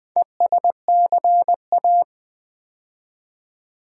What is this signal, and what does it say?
Morse code, heard as a keyed tone, about 20 wpm, 700 Hz. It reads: ESCA
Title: Build A Word - 2 Letters At A Time - 8-Letter Words - 20wpm